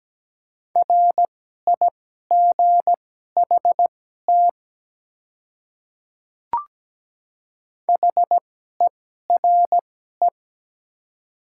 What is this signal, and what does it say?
Morse code, heard as a keyed tone, about 17 wpm, spaced as for 12 wpm, 700 Hz. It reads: RIGHT HERE
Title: 2x Character Spacing - Top 200 Words - 17wpm